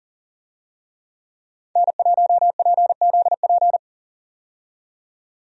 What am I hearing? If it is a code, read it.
Morse code, heard as a keyed tone, about 40 wpm, 700 Hz. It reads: N1PZP
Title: USA Call Signs - 5 Characters - 40wpm